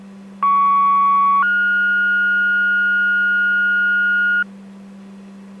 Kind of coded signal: two-tone page